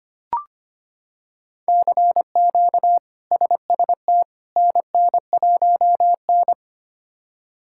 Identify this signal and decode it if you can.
Morse code, heard as a keyed tone, about 25 wpm, 700 Hz. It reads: CQ SST NN1N
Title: K1USN SST QSO Practice - 25wpm